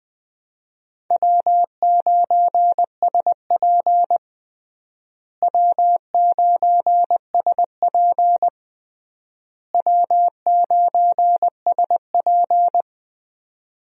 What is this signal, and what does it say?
Morse code, heard as a keyed tone, about 21 wpm, 700 Hz. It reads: W9SP W9SP W9SP
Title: USA Call Signs - 4 Characters - Repeated 3x - 20wpm